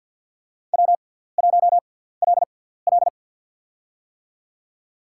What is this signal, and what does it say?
Morse code, heard as a keyed tone, about 53 wpm, spaced as for 17 wpm, 700 Hz. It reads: W1LL
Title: USA Call Signs - 6x Character Spacing - 50wpm